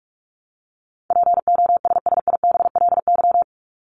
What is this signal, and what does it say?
Morse code, heard as a keyed tone, about 44 wpm, 700 Hz. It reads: POSSIBLY
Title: Top 1251-1500 Words 45wpm